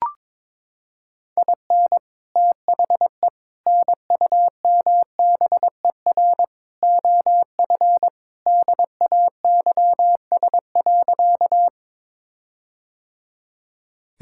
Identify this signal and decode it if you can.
Morse code, heard as a keyed tone, about 22 wpm, 700 Hz. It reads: IN THE NUMBER OF DAYS.
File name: practice / 22wpm / Sentences with 4-Word Phrases - 22wpm